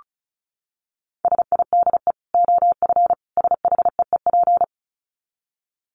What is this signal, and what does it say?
Morse code, heard as a keyed tone, about 34 wpm, 700 Hz. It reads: SIDE OF SHEEP
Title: Dit Challenge 35wpm